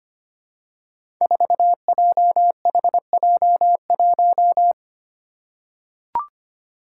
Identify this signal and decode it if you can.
Morse code, heard as a keyed tone, about 25 wpm, 700 Hz. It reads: 4JHJ1
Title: Five Challenging Letter-Number Combinations 25wpm